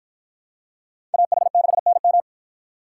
Morse code, heard as a keyed tone, about 49 wpm, 700 Hz. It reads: AH6NM